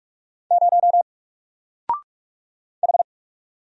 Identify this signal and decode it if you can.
Morse code, heard as a keyed tone, about 45 wpm, 700 Hz. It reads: 0 H
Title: Single Letter-Number 45wpm